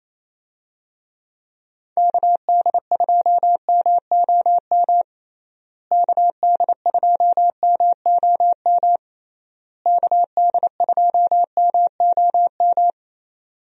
Morse code, heard as a keyed tone, about 27 wpm, 700 Hz. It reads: KD2MOM KD2MOM KD2MOM
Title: USA Call Signs - 6 Characters - Repeated 3x - 28wpm